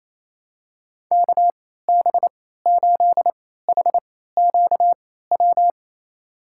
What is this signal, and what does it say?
Morse code, heard as a keyed tone, about 28 wpm, spaced as for 16 wpm, 700 Hz. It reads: KB8HQW
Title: USA Call Signs - 6 Characters - 3x Character Spacing - 28wpm